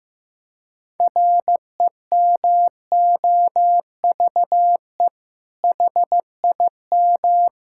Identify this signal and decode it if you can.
Morse code, heard as a keyed tone, about 15 wpm, 700 Hz. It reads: REMOVE HIM
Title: Sentences from Top 1000 Words - Review Part 5 - 15wpm